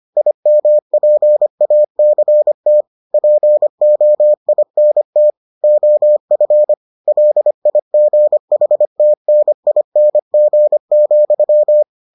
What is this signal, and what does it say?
Morse code, heard as a keyed tone, about 25 wpm, 600 Hz. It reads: IMPACT POINT OF LIGHTNING,